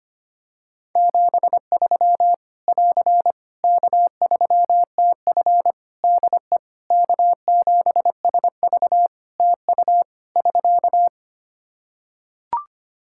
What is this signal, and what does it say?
Morse code, heard as a keyed tone, about 25 wpm, 700 Hz. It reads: 73 <AR> K3TF DE K7SV TU <SK>
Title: Practice QSO - Exchange 4 - Ending - 25wpm